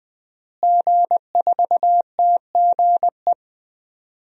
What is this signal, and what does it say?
Morse code, heard as a keyed tone, about 21 wpm, 700 Hz. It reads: G4TGE
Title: DXCC-100 Call Signs - 5 Characters - Repeated 3x - 20wpm